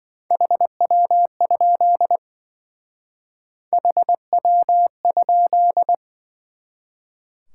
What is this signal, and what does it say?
Morse code, heard as a keyed tone, about 25 wpm, 700 Hz. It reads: HW? HW?